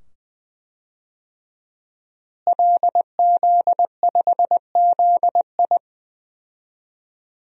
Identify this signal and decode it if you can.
Morse code, heard as a keyed tone, about 20 wpm, 700 Hz. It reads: LZ5ZI
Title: DXCC-100 Call Signs - 5 Characters - Repeated 2x - 20wpm